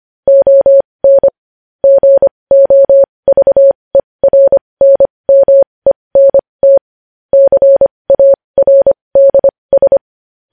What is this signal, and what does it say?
Morse code, heard as a keyed tone, about 25 wpm, spaced as for 20 wpm, 550 Hz. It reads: ON GOVERNMENT CARDS
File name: podcast / News headlines-1x-25x20